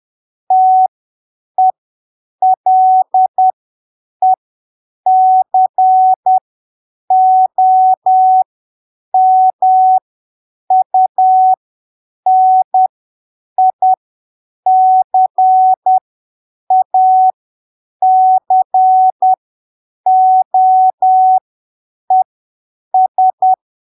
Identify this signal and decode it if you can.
Morse code, heard as a keyed tone, about 10 wpm, spaced as for 7 wpm, 750 Hz. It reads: TELECOMUNICACOES
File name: arquivos / texto / texto 25 10ppm